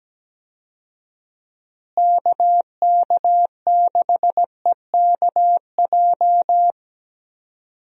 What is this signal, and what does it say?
Morse code, heard as a keyed tone, about 17 wpm, 700 Hz. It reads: KK6EKJ
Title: USA Call Signs - 6 Characters - Repeated 3x - 17wpm